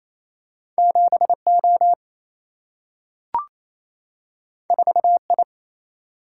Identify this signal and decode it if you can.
Morse code, heard as a keyed tone, about 28 wpm, 700 Hz. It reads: 7 O 4 I
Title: Single-Letter Single-Number 28wpm